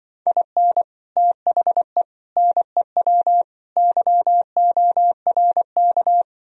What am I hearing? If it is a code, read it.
Morse code, heard as a keyed tone, about 24 wpm, 700 Hz. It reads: IN THE NEW YORK
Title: Speed-Racing - Sets of 4 Words 20wpm